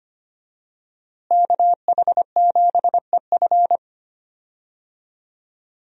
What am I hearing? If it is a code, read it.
Morse code, heard as a keyed tone, about 25 wpm, 700 Hz. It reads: KH7EF
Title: USA Call Signs - 5 Characters - Repeated 2x - 25wpm